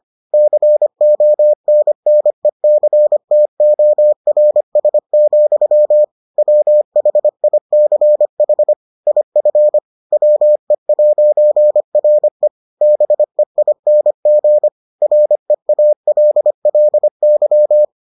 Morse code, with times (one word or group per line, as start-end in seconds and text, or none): 0.3-6.0 CONNECTORS,
6.4-8.7 WHICH
9.1-9.8 IF
10.1-12.5 WE'RE
12.8-14.7 BEING
15.0-17.9 REALLY